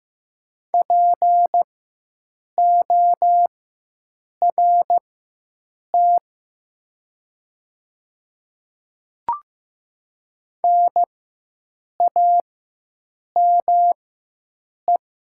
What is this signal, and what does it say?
Morse code, heard as a keyed tone, about 15 wpm, spaced as for 7 wpm, 700 Hz. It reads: PORT NAME
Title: 4x Character Spacing - Top 200 Words - 15wpm